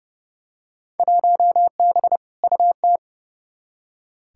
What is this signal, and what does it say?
Morse code, heard as a keyed tone, about 29 wpm, 700 Hz. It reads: 1B UT
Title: Field Day Exchanges - 30wpm